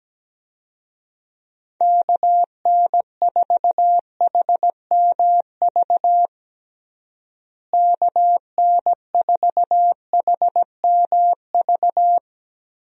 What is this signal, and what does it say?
Morse code, heard as a keyed tone, about 17 wpm, 700 Hz. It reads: KN4HMV KN4HMV
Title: USA Call Signs - 6 Characters - Repeated 3x - 17wpm